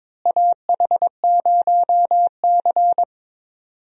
Morse code, heard as a keyed tone, about 22 wpm, 700 Hz. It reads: AH0C
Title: USA Call Signs - 4 Characters - Repeated 2x - 22wpm